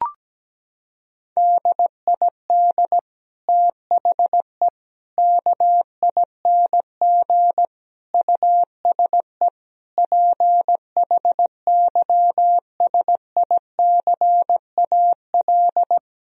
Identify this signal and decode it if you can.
Morse code, heard as a keyed tone, about 17 wpm, 700 Hz. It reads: DID THE KING USE PHYSICAL
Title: Sentences from Top 1000 Words - Review Part 4 - 17wpm